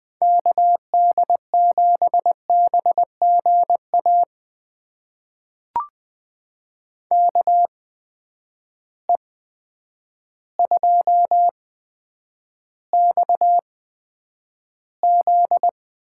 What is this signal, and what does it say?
Morse code, heard as a keyed tone, about 20 wpm, spaced as for 9 wpm, 700 Hz. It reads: KD7BGA KE2XZ
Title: USA Call Signs - 8x Character Spacing - 20wpm